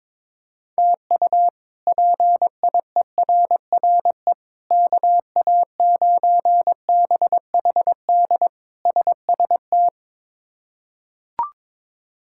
Morse code, heard as a keyed tone, about 22 wpm, 700 Hz. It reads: TU PIERRE KA9BHD SST
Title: K1USN SST QSO Practice - 22wpm